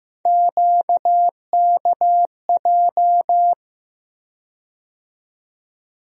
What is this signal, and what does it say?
Morse code, heard as a keyed tone, about 15 wpm, 700 Hz. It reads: QKJ
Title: USA Call Sign Postfixes - 3 Letters 15wpm